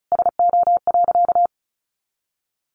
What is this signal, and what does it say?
Morse code, heard as a keyed tone, about 34 wpm, 700 Hz. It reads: SO.
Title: Sentences from Top 1100 Words 35wpm